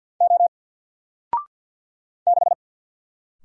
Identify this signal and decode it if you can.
Morse code, heard as a keyed tone, about 48 wpm, 700 Hz. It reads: O 6
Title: Single Character (Letter:Number:Punctuation:Prosign) Alt 50wpm